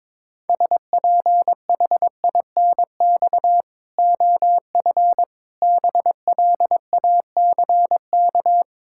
Morse code, heard as a keyed tone, about 22 wpm, 700 Hz. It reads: SPHINX OF BLACK
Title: Pangram Challenge v2 - 22wpm